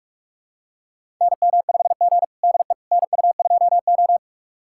Morse code, heard as a keyed tone, about 41 wpm, 700 Hz. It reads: NM5G DE NU2O
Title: USA Callsign DE Callsign - 4 Characters - 45wpm